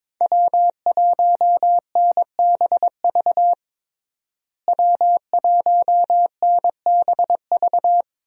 Morse code, heard as a keyed tone, about 22 wpm, 700 Hz. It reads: W1NBV W1NBV
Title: USA Call Signs - 5 Characters - Repeated 2x - 22wpm